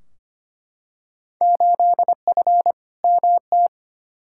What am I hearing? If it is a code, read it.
Morse code, heard as a keyed tone, about 25 wpm, 700 Hz. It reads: 8F MT
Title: Field Day Exchanges Encore - 25wpm